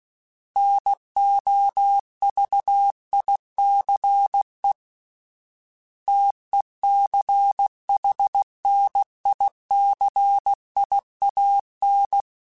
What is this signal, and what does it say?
Morse code, heard as a keyed tone, about 16 wpm, 800 Hz. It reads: NOVICE TECHNICIAN